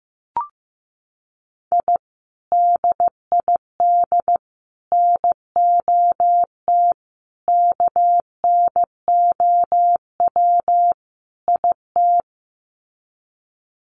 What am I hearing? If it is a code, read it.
Morse code, heard as a keyed tone, about 15 wpm, 700 Hz. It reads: I DID NOT KNOW IT
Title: Sets of 5 Words Encore2 15wpm